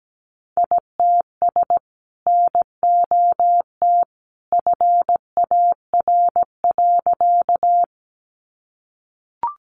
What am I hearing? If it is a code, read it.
Morse code, heard as a keyed tone, about 17 wpm, 700 Hz. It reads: ITS NOT FAR.
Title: Sentences from Top 600 Words 17wpm